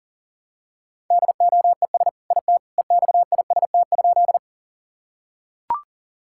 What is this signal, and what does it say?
Morse code, heard as a keyed tone, about 40 wpm, 700 Hz. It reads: DOES IT EXIST?
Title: Sentences from Top 1000 Words - Review Part 5 - 40wpm